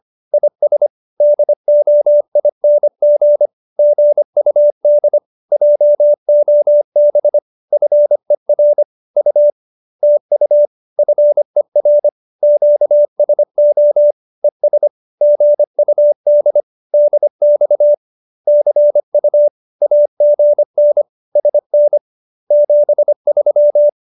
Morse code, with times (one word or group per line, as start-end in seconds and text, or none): 0.3-0.9 IS
1.2-3.5 DOING
3.8-5.2 GUD
5.5-7.4 JOB
7.7-8.8 FER
9.2-9.5 U
10.0-10.7 TU
11.0-12.1 FER
12.4-14.1 QSO
14.4-14.9 ES
15.2-16.6 GUD
16.9-17.9 DX
18.5-19.5 CU
19.8-21.0 AGN
21.4-22.0 SN
22.5-23.9 73